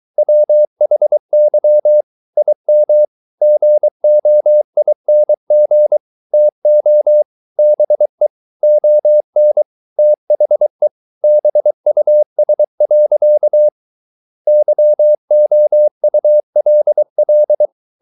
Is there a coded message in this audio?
Yes: Morse code from 0.2 to 17.7 s, heard as a keyed tone, about 23 wpm, 600 Hz. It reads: WHY IM GOING TO BE ON THE BUS. YOULL